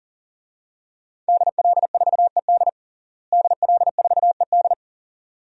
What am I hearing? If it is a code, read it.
Morse code, heard as a keyed tone, about 40 wpm, 700 Hz. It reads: DL4ED DL4ED
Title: DXCC-100 Call Signs - 5 Characters - Repeated 2x - 40wpm